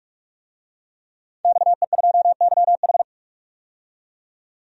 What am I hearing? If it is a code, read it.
Morse code, heard as a keyed tone, about 45 wpm, 700 Hz. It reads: XE2YH
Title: DXCC-100 Call Signs - 5 Characters - 45wpm